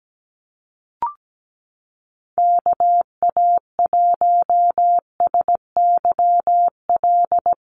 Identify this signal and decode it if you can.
Morse code, heard as a keyed tone, about 17 wpm, 700 Hz. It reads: KA1SYL
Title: USA Call Signs Repeated 3x 17wpm